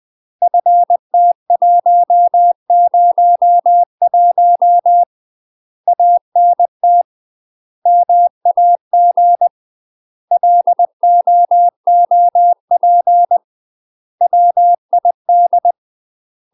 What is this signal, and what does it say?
Morse code, heard as a keyed tone, about 20 wpm, 700 Hz. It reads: FT101 ANT MAG LOOP WID
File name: UK MASTER QSOs TRAINING / 20WPM / UK MASTER QSOs 0047